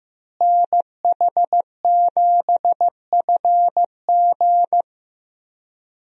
Morse code, heard as a keyed tone, about 15 wpm, 700 Hz. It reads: NH7FG
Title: USA Call Signs - 5 Characters - Repeated 2x - 15wpm